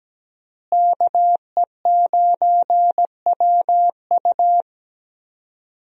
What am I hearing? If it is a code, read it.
Morse code, heard as a keyed tone, about 17 wpm, 700 Hz. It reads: KE9WU
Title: USA Call Signs - 5 Characters - Repeated 2x - 17wpm